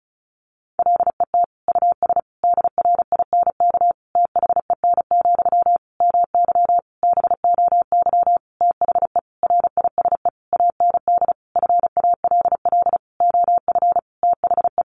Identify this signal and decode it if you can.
Morse code, heard as a keyed tone, about 34 wpm, 700 Hz. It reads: LET US DRINK THEN, MY BOY THE RISE AND FALL OF THE
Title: Sentences from Top 1500 Words 35wpm